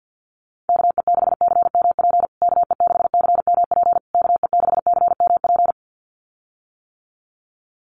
Morse code, heard as a keyed tone, about 48 wpm, 700 Hz. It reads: KE6CMP KE6CMP KE6CMP
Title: USA Call Signs Repeated 3x 50wpm